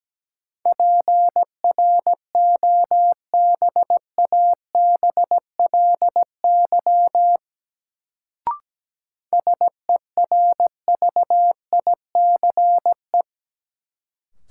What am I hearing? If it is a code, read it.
Morse code, heard as a keyed tone, about 17 wpm, 700 Hz. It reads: PROBABLY SERVICE
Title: Sentences from Top 700 Words - New Word Drill - 17wpm